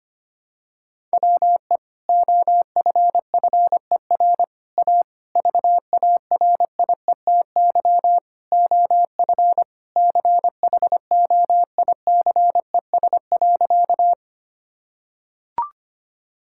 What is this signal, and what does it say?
Morse code, heard as a keyed tone, about 25 wpm, 700 Hz. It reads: WE OFFER A VARIETY OF CHOICES.
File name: practice / 25wpm / Sentences with 3-Word Phrases - 25wpm